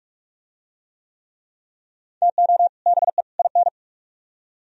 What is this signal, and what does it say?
Morse code, heard as a keyed tone, about 40 wpm, 700 Hz. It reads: TO BE IN